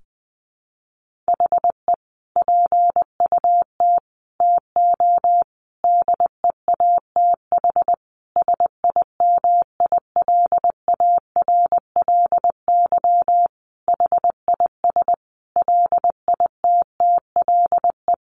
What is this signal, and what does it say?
Morse code, heard as a keyed tone, about 19 wpm, 700 Hz. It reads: HE PUT TO DEATH SIMILARLY HIS LITTLE
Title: Sentences from Top 1500 Words 20wpm